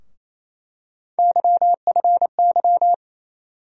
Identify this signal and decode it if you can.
Morse code, heard as a keyed tone, about 28 wpm, 700 Hz. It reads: YFY
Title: USA Call Sign Postfixes - 3 Letters 28wpm